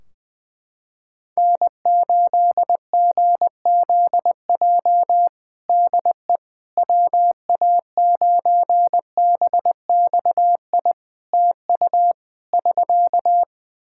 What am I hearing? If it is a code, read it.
Morse code, heard as a keyed tone, about 20 wpm, 700 Hz. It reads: N8GZJ DE WA9BXI TU <SK>
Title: K1USN SST QSO Practice 20wpm